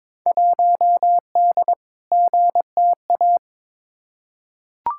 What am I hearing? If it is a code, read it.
Morse code, heard as a keyed tone, about 22 wpm, 700 Hz. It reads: 1D GTA
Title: Field Day Exchanges - 22wpm